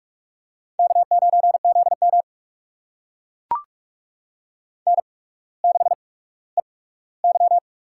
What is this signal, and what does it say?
Morse code, heard as a keyed tone, about 44 wpm, spaced as for 19 wpm, 690 Hz. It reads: K9ZM N6EY
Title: USA Call Signs - 4 Characters - 8x Character Spacing - 45wpm